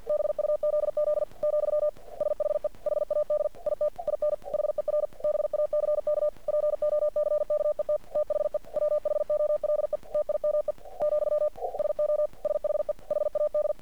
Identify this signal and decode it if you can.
Morse code, heard as a keyed tone, about 50 wpm, 600 Hz. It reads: BUZZ , SHE HAD IT IN HER BACK POCKET THE WHOLE TIME , SO SHE HAD
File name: archived / CRSnet